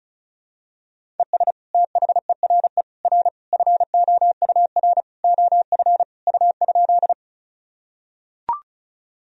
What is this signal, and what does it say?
Morse code, heard as a keyed tone, about 36 wpm, 700 Hz. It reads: ES THERE R FOUR OF U?
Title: Sentences from Top 300 Words and Abbreviations 35wpm